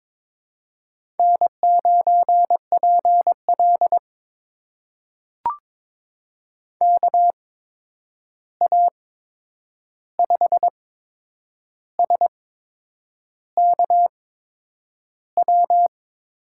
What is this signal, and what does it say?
Morse code, heard as a keyed tone, about 22 wpm, spaced as for 8 wpm, 700 Hz. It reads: N9PL KA5SKW
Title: USA Call Signs - 8x Character Spacing - 22wpm